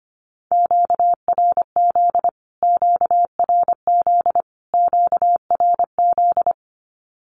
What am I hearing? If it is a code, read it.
Morse code, heard as a keyed tone, about 25 wpm, 700 Hz. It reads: QRZ QRZ QRZ
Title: Q-Codes Repeated 3x 25wpm